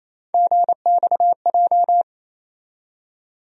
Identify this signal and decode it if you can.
Morse code, heard as a keyed tone, about 27 wpm, 700 Hz. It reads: GXJ
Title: USA Call Sign Postfixes - 3 Letters 28wpm